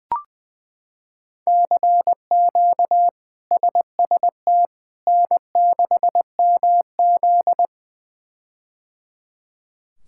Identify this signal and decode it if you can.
Morse code, heard as a keyed tone, about 20 wpm, 700 Hz. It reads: CQ SST N6MZ
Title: K1USN SST QSO Practice - 20wpm